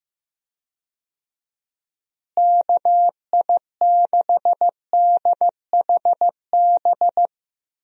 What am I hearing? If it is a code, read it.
Morse code, heard as a keyed tone, about 15 wpm, 700 Hz. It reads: KI6DHB
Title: USA Call Signs - 6 Characters - Repeated 3x - 15wpm